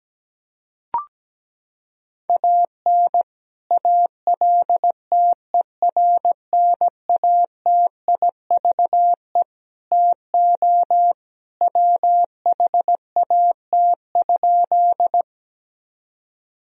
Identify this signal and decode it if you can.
Morse code, heard as a keyed tone, about 17 wpm, 700 Hz. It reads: AN ALTERNATIVE TO WHAT?
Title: Sentences from Top 1000 Words - Review Part 4 - 17wpm